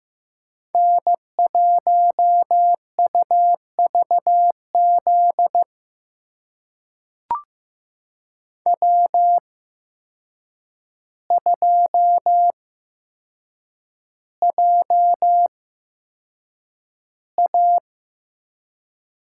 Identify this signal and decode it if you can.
Morse code, heard as a keyed tone, about 15 wpm, spaced as for 7 wpm, 700 Hz. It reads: N1UVZ W2JA
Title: USA Call Signs - 5 Characters - 8x Character Spacing - 15wpm